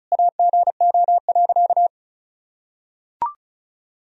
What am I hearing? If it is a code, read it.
Morse code, heard as a keyed tone, about 36 wpm, 700 Hz. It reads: AGO.
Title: Sentences from Top 1000 Words - Review Part 1 - 35wpm